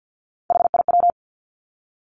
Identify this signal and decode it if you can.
Morse code, heard as a keyed tone, about 50 wpm, 700 Hz. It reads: HIW
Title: Three Challenging Letter-Number Combinations 50wpm